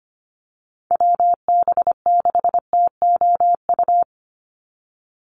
Morse code, heard as a keyed tone, about 25 wpm, 700 Hz. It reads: WB6TOU
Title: Call Signs Repeated 2x - 25wpm